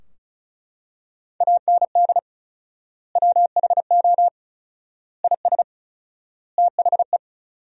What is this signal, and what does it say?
Morse code, heard as a keyed tone, about 34 wpm, 700 Hz. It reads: AND WHO IS THE
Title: 4x Word-Spacing - Sentences from Top 200 Words - 35wpm